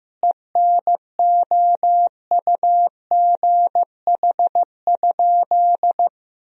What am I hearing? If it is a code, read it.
Morse code, heard as a keyed tone, about 15 wpm, 700 Hz. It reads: ENOUGH?
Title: Sentences from Top 1000 Words - Review Part 2 - 15wpm